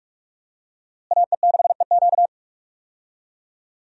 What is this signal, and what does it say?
Morse code, heard as a keyed tone, about 41 wpm, 700 Hz. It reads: AE6EQ
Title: USA Call Signs - 5 Characters - Repeated 3x - 45wpm